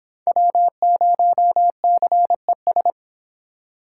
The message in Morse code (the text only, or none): W0CES